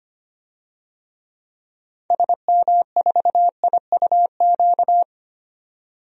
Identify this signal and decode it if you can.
Morse code, heard as a keyed tone, about 25 wpm, 700 Hz. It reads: SM4IUQ
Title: DXCC-100 Call Signs - 6 Characters - 25wpm